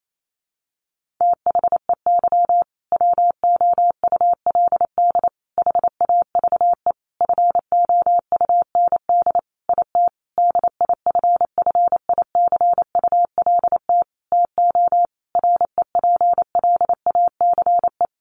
Morse code, heard as a keyed tone, about 28 wpm, 700 Hz. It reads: THEY WOULD HAVE FOUND IT DIFFICULT TO REPLACE
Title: Sentences from Top 1400 Words 28wpm